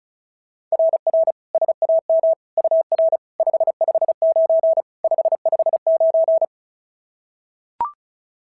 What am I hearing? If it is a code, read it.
Morse code, heard as a keyed tone, about 36 wpm, 690 Hz. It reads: RR SAM UR 559 559
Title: SOTA QSO Practice 35wpm